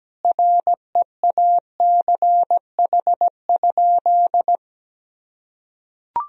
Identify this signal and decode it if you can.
Morse code, heard as a keyed tone, about 17 wpm, 700 Hz. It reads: REACH?
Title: Sentences from Top 1000 Words - Review Part 4 - 17wpm